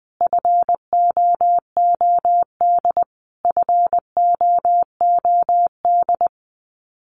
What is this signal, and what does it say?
Morse code, heard as a keyed tone, about 20 wpm, 700 Hz. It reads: FOOD FOOD
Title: Top 300 Words Repeated 20wpm